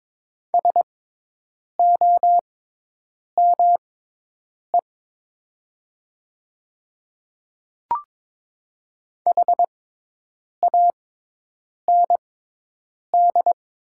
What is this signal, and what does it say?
Morse code, heard as a keyed tone, about 22 wpm, spaced as for 8 wpm, 700 Hz. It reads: SOME HAND
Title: 6x Character Spacing - Top 200 Words - 22wpm